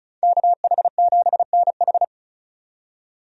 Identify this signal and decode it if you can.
Morse code, heard as a keyed tone, about 34 wpm, 700 Hz. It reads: KH7NH
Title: USA Call Signs - 5 Characters - 35wpm